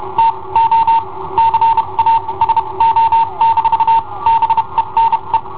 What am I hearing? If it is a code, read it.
Morse code, heard as a keyed tone, about 30 wpm, spaced as for 24 wpm, 910 Hz. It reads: TO CASO-BENE
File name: i7ohp / cassetta i7ohp B